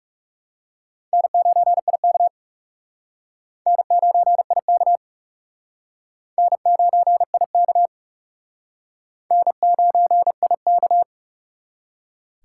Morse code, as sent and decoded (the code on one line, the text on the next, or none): -. ----. .. -.- / -. ----. .. -.- / -. ----. .. -.- / -. ----. .. -.-
N9IK N9IK N9IK N9IK